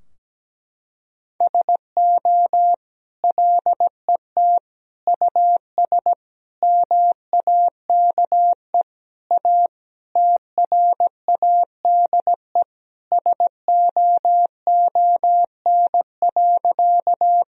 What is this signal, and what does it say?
Morse code, heard as a keyed tone, about 17 wpm, 700 Hz. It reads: SO LET US MAKE A TRADE SOON.
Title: Sentences from Top 1000 Words - Review Part 4 - 17wpm